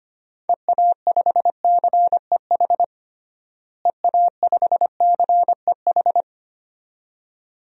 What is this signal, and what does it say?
Morse code, heard as a keyed tone, about 25 wpm, 700 Hz. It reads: EA5CEH EA5CEH
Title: DXCC-100 Call Signs - 6 Characters - Repeated 2x - 25wpm